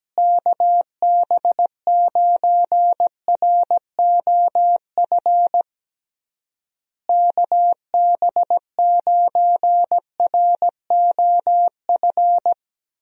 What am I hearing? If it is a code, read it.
Morse code, heard as a keyed tone, about 17 wpm, 700 Hz. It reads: KB9ROF KB9ROF
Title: USA Call Signs - 6 Characters - Repeated 3x - 17wpm